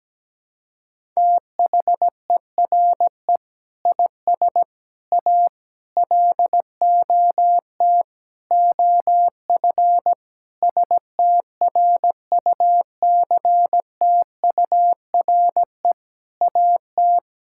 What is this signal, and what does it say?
Morse code, heard as a keyed tone, about 17 wpm, 700 Hz. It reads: THERE IS A LOT OF STRUCTURE AT
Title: Sentences from Top 1000 Words - Review Part 6 - 17wpm